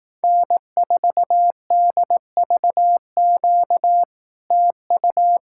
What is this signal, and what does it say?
Morse code, heard as a keyed tone, about 18 wpm, 700 Hz. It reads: N4DVQ TU
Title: Speed-Racing Common QSO Phrases Encore - 15wpm